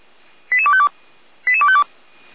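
Five-tone selective call (ZVEI), heard as quick runs of five tones, tones of about 70 ms. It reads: 88252 88252
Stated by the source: ZVEI five-tone call received on SDR, audio.f85360.feuerwehr-orlamuende.org fire brigade server.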